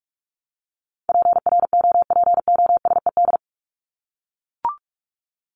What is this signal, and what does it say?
Morse code, heard as a keyed tone, about 44 wpm, 700 Hz. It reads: PROPOSED
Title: ED Suffix - Top 100 Words with Base Words 45wpm